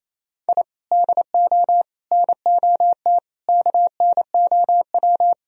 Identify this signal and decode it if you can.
Morse code, heard as a keyed tone, about 34 wpm, 700 Hz. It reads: I DO NOT KNOW
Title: Speed-Racing - Sets of 5 Words 28wpm